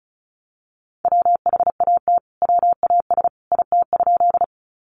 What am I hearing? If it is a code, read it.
Morse code, heard as a keyed tone, about 34 wpm, 700 Hz. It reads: WHAT WAS IT?
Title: Sentences from Top 100 Words 35wpm